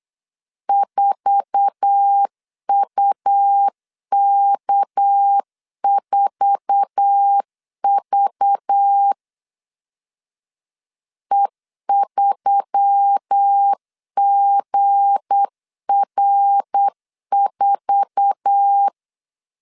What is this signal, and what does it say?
Morse code, heard as a keyed tone, about 8 wpm, 800 Hz. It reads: 4UK4V E3GR4
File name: LEZIONI / 8 YJ45 / 8 YJ4 8wpm 3246